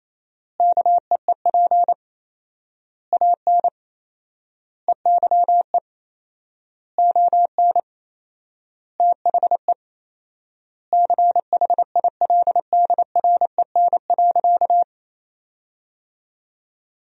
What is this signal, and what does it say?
Morse code, heard as a keyed tone, about 28 wpm, 700 Hz. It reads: KEEP AN EYE ON THE CHILDREN.
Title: 4x Word-Spacing - Sentences from Top 300 Words - 28wpm